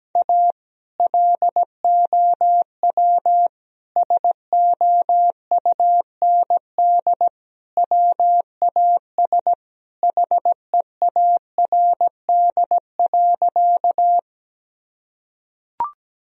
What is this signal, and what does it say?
Morse code, heard as a keyed tone, about 17 wpm, 700 Hz. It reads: A LOW SOUND WAS HEARD.
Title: Sentences from Top 1000 Words - Review Part 1 - 17wpm